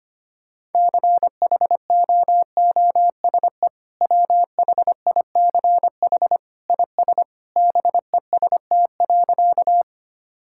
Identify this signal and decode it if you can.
Morse code, heard as a keyed tone, about 25 wpm, 700 Hz. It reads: CHOOSE WHICH IS BEST.
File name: practice / 25wpm / Sentences with 2-Word Phrases - 25wpm